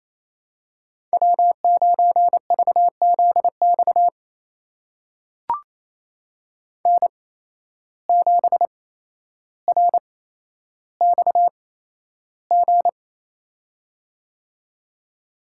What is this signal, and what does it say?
Morse code, heard as a keyed tone, about 28 wpm, spaced as for 12 wpm, 700 Hz. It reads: W9VZX N7RXG
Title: USA Call Signs - 5 Characters - 8x Character Spacing - 28wpm